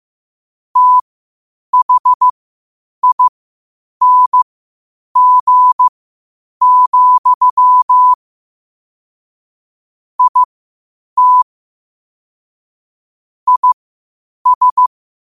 Morse code, heard as a keyed tone, about 15 wpm, spaced as for 8 wpm, 1000 Hz. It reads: THING, IT IS
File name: qotdfiles / 8wpm